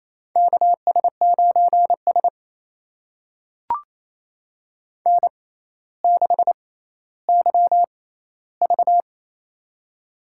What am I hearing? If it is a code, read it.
Morse code, heard as a keyed tone, about 28 wpm, spaced as for 14 wpm, 700 Hz. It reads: KS9S N6YV
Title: USA Call Signs - 4 Characters - 6x Character Spacing - 28wpm